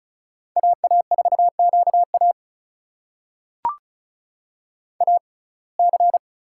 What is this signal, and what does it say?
Morse code, heard as a keyed tone, about 36 wpm, spaced as for 26 wpm, 700 Hz. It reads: AA4QA AC
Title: USA Call Signs - 5 Characters - 6x Character Spacing - 35wpm